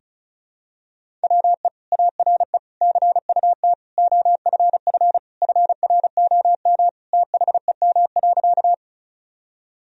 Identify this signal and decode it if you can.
Morse code, heard as a keyed tone, about 36 wpm, 700 Hz. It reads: WE ARE CUT OFF FROM THEM.
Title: Sentences from Top 1000 Words - Review Part 1 - 35wpm